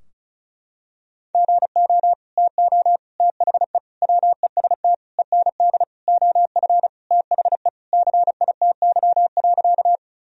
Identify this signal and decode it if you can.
Morse code, heard as a keyed tone, about 36 wpm, 700 Hz. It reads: GO TO THE WEST END OF THE CITY.
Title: Sentences from Top 1000 Words - Review Part 2 - 35wpm